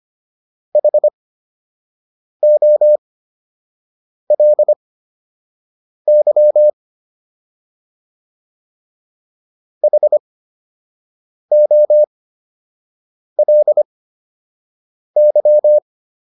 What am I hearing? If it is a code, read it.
Morse code, heard as a keyed tone, about 25 wpm, spaced as for 6 wpm, 600 Hz. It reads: HOLY HOLY